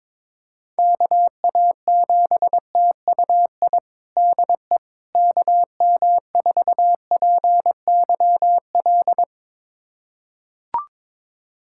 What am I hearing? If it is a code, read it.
Morse code, heard as a keyed tone, about 22 wpm, 700 Hz. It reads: KA7TUI DE KM4PYL
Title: USA Callsign DE Callsign 22wpm